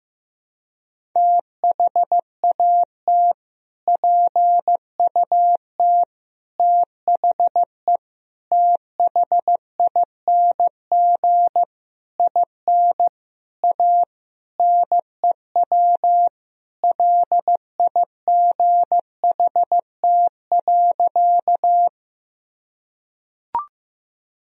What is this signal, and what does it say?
Morse code, heard as a keyed tone, about 15 wpm, 700 Hz. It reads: THAT PUT THE THING IN A NEW LIGHT.
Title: Sentences from Top 1000 Words - Review Part 1 - 15wpm